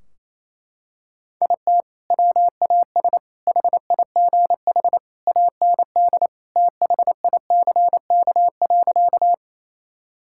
Morse code, heard as a keyed tone, about 28 wpm, 700 Hz. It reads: IT WAS HIGH AND THICK.
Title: Sentences from Top 1600 Words 28wpm